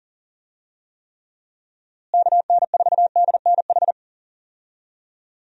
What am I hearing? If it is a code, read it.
Morse code, heard as a keyed tone, about 40 wpm, 700 Hz. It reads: KN4DNH
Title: USA Call Signs - 6 Characters - Repeated 3x - 40wpm